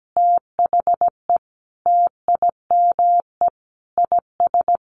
Morse code, heard as a keyed tone, about 17 wpm, 700 Hz. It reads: THE TIME IS
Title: Sentences from Top 1400 Words 17wpm